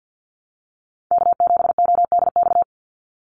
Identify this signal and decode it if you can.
Morse code, heard as a keyed tone, about 50 wpm, 700 Hz. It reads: K7ODX
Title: NA SOTA Chaser Callsigns 50wpm